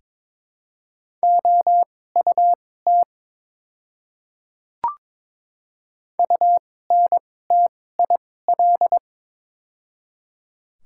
Morse code, heard as a keyed tone, about 22 wpm, spaced as for 16 wpm, 700 Hz. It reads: OUT UNTIL
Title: 2x Character Spacing - Top 300 Words - 22wpm